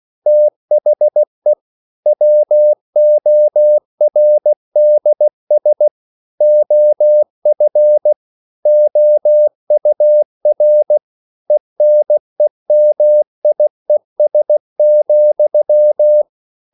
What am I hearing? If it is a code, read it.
Morse code, heard as a keyed tone, about 16 wpm, 600 Hz. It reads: THE WORDS OF OUR ENEMIES,
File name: CW / Practice / EnglishQuotations0116